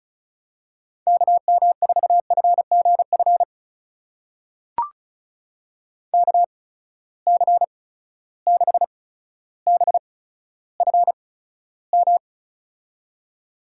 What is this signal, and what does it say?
Morse code, heard as a keyed tone, about 36 wpm, spaced as for 15 wpm, 700 Hz. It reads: KM4FGF KC6BFM
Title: USA Call Signs - 6 Characters - 8x Character Spacing - 35wpm